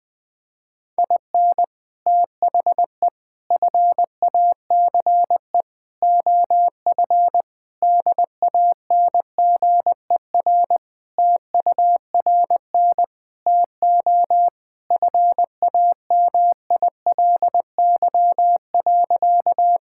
Morse code, heard as a keyed tone, about 20 wpm, 700 Hz. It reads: IN THE FACE OF DANGER TURN TO FAMILY.
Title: Sentences with 3-Word Phrases - 20wpm